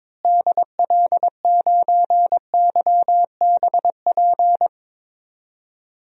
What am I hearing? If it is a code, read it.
Morse code, heard as a keyed tone, about 22 wpm, 700 Hz. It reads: DL9YBP
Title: DXCC-100 Call Signs - 6 Characters - Repeated 2x - 22wpm